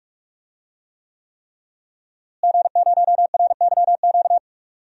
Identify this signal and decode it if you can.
Morse code, heard as a keyed tone, about 46 wpm, 700 Hz. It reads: G0RYQ
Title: DXCC-100 Call Signs - 5 Characters - Repeated 3x - 45wpm